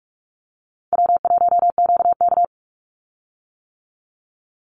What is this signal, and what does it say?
Morse code, heard as a keyed tone, about 46 wpm, 700 Hz. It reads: W1QK
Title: Easy Call Signs - 45wpm